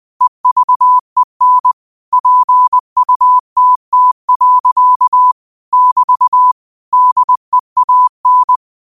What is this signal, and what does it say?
Morse code, heard as a keyed tone, about 20 wpm, 1000 Hz. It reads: EVEN PUTT. <BT> DEAN